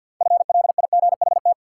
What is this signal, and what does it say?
Morse code, heard as a keyed tone, about 53 wpm, 700 Hz. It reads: FLIGHT
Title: Sentences from Top 1000 Words - New Word Drill - 50wpm